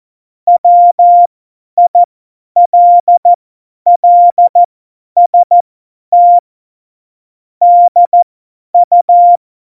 Morse code, heard as a keyed tone, about 14 wpm, spaced as for 10 wpm, 700 Hz. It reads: WILLST DU
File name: Morsekurs / Witze / Witz 17 April 2018, 50 F70